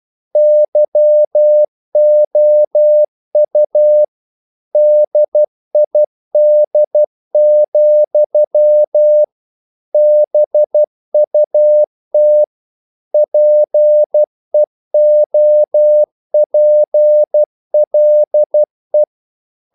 Morse code, heard as a keyed tone, about 12 wpm, 600 Hz. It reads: YOU DID, BUT PEOPLE
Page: English quotations in CW (telegraphy - morse)